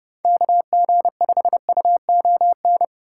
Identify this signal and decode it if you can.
Morse code, heard as a keyed tone, about 40 wpm, spaced as for 35 wpm, 700 Hz. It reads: KG5UON
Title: Speed-Racing Common QSO Phrases - 30wpm